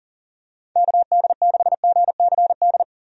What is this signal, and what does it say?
Morse code, heard as a keyed tone, about 40 wpm, 700 Hz. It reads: KD6GCD
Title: USA Call Signs - 6 Characters - Repeated 2x - 40wpm